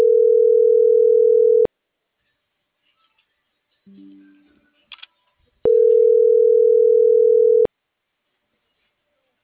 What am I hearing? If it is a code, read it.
Ringback tone (North America), 440 + 480 Hz.